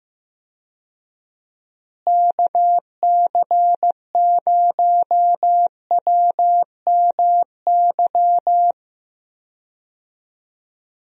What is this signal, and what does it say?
Morse code, heard as a keyed tone, about 16 wpm, 700 Hz. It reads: KC0WMY
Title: USA Call Signs - 6 Characters - Repeated 2x - 15wpm